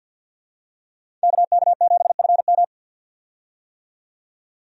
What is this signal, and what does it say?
Morse code, heard as a keyed tone, about 48 wpm, 700 Hz. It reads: KK7FM
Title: USA Call Signs - 5 Characters - 50wpm